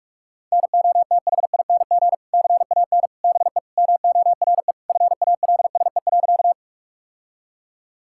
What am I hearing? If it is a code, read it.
Morse code, heard as a keyed tone, about 46 wpm, 700 Hz. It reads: NOTHING CAN BE MORE FALSE.
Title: Sentences from Top 1600 Words 45wpm